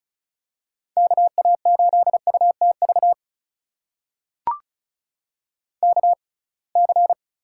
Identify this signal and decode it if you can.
Morse code, heard as a keyed tone, about 36 wpm, spaced as for 27 wpm, 700 Hz. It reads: KA8UTV KC
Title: USA Call Signs - 6 Characters - 6x Character Spacing - 35wpm